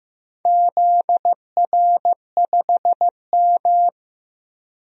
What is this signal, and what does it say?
Morse code, heard as a keyed tone, about 15 wpm, 700 Hz. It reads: ZR5M
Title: DXCC-100 Call Signs - 4 Characters - Repeated 3x - 15wpm